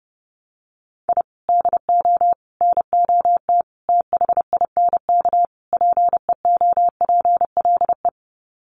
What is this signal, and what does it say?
Morse code, heard as a keyed tone, about 30 wpm, 700 Hz. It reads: I DO NOT THINK PEOPLE
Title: Sets of 5 Words Encore2 30wpm